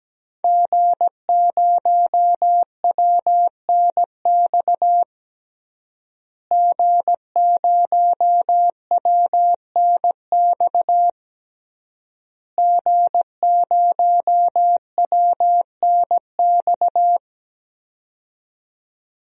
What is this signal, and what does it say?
Morse code, heard as a keyed tone, about 17 wpm, 700 Hz. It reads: G0WNX G0WNX G0WNX
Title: DXCC-100 Call Signs - 5 Characters - Repeated 3x - 17wpm